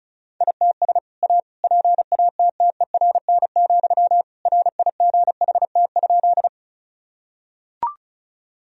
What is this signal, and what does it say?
Morse code, heard as a keyed tone, about 34 wpm, 700 Hz. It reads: ITS A PATTERN, RIGHT?
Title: Sentences from Top 1000 Words - Review Part 2 - 35wpm